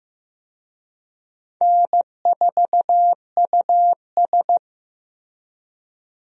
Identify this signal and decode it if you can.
Morse code, heard as a keyed tone, about 15 wpm, 700 Hz. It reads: N4US
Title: USA Call Signs - 4 Characters - Repeated 3x - 15wpm